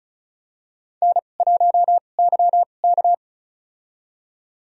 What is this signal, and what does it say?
Morse code, heard as a keyed tone, about 34 wpm, spaced as for 25 wpm, 700 Hz. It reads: N1YK
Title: USA Call Signs - 2x Character Spacing - 35wpm